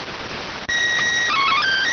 SSTV (slow-scan television), PD 290.